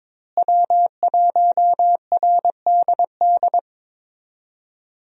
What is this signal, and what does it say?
Morse code, heard as a keyed tone, about 22 wpm, 700 Hz. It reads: W1RDD
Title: USA Call Signs - 5 Characters - Repeated 3x - 22wpm